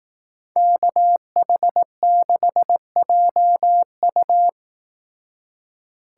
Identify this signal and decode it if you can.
Morse code, heard as a keyed tone, about 18 wpm, 700 Hz. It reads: KH6JU
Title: Speed Racing - USA 2x2 Call Signs 15wpm